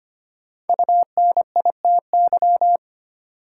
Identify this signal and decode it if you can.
Morse code, heard as a keyed tone, about 25 wpm, 700 Hz. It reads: UNITY
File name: 5 Letter Words - First 3 Letters Given - 25wpm